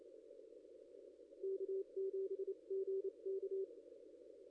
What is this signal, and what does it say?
Morse code, heard as a keyed tone, about 30 wpm, spaced as for 26 wpm, 390 Hz. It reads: K7GK
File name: contests / ss-cw-k5tr-op-n5rz -0400